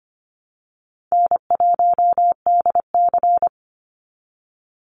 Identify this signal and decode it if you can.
Morse code, heard as a keyed tone, about 25 wpm, 700 Hz. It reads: N1DC
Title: Easy Call Signs Repeated 2x - 25wpm